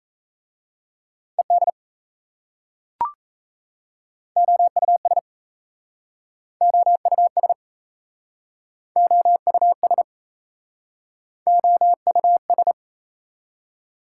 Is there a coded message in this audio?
Yes: Morse code from 1.4 to 12.7 s, heard as a keyed tone, about 37 wpm, 700 Hz. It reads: ED OUS OUS OUS OUS